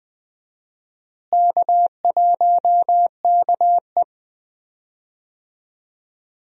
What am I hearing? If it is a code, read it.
Morse code, heard as a keyed tone, about 21 wpm, 700 Hz. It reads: K1KE
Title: USA Call Signs - 4 Characters - Repeated 3x - 20wpm